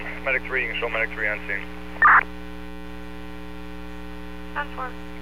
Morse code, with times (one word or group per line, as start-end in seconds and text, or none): none